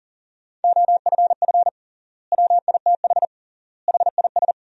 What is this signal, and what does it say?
Morse code, heard as a keyed tone, about 40 wpm, 700 Hz. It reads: OFF WITH HIS
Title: 3x Word-Spacing - Sentences from Top 300 Words - 40wpm